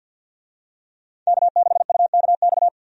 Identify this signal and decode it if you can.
Morse code, heard as a keyed tone, about 53 wpm, 700 Hz. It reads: K6UKX
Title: USA Call Signs - 5 Characters - 50wpm